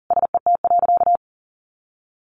Morse code, heard as a keyed tone, about 40 wpm, 700 Hz. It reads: SET.
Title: Additional Easy Phrases 40wpm